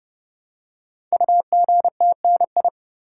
Morse code, heard as a keyed tone, about 29 wpm, 700 Hz. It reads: UGTNI
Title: Code Groups - 5 Letters - 30wpm